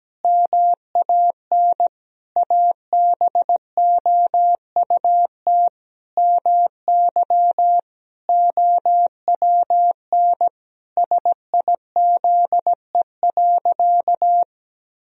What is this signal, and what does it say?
Morse code, heard as a keyed tone, about 17 wpm, 700 Hz. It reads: MAN ABOUT MY OWN SIZE.
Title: Sentences from Top 1000 Words - Review Part 2 - 17wpm